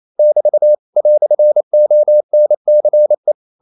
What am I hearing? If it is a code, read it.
Morse code, heard as a keyed tone, about 28 wpm, 600 Hz. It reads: <BT> "ONCE
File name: CW / Practice / EnglishQuotations0228